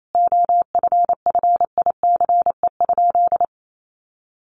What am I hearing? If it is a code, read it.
Morse code, heard as a keyed tone, about 28 wpm, 700 Hz. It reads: OFFICE?
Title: Sentences from Top 1000 Words 28wpm